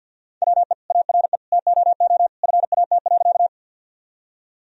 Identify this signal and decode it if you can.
Morse code, heard as a keyed tone, about 50 wpm, 700 Hz. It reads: WE ARE TOO FAT.
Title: Sentences from Top 1000 Words - Review Part 4 - 50wpm